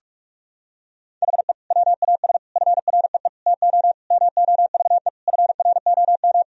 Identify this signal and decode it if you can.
Morse code, heard as a keyed tone, about 46 wpm, 700 Hz. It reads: HE WAS FREE TO MOVE FROM
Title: Sentences from Top 1000 Words - Review Part 1 - 45wpm